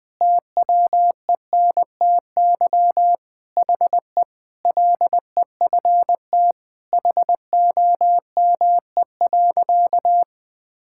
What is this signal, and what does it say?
Morse code, heard as a keyed tone, about 20 wpm, 700 Hz. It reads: TWENTY HE LEFT HOME.
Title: Sentences with 4-Word Phrases - Spoken Beginning - 20wpm